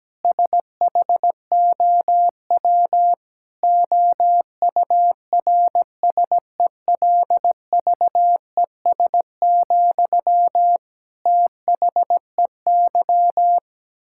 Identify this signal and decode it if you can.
Morse code, heard as a keyed tone, about 17 wpm, 700 Hz. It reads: SHOW OURSELVES, THEY
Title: Sentences from Top 1600 Words 17wpm